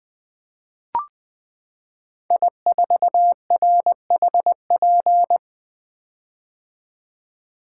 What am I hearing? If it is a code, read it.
Morse code, heard as a keyed tone, about 20 wpm, 700 Hz. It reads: I4RHP
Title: 2023 POTA Hunter Call Signs - 20wpm